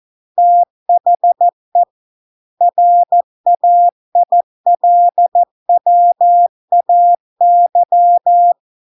Morse code, heard as a keyed tone, about 14 wpm, 700 Hz. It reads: THE RAILWAY